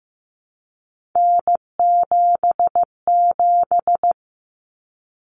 Morse code, heard as a keyed tone, about 15 wpm, 700 Hz. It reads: N77
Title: Common Cut Numbers - 3-Digit 15wpm